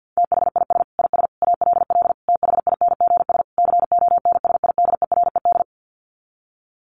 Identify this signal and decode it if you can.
Morse code, heard as a keyed tone, about 50 wpm, 700 Hz. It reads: THIS IS ALL THINGS CONSIDERED
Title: Sets of 5 Words 50wpm